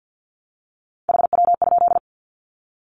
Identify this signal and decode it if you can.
Morse code, heard as a keyed tone, about 50 wpm, 700 Hz. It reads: HW?
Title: Common QSO Phrases 50wpm